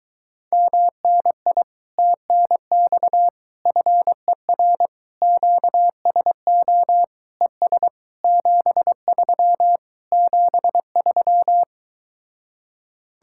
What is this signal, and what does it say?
Morse code, heard as a keyed tone, about 23 wpm, 700 Hz. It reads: MNI TNX FER QSO ES 73 73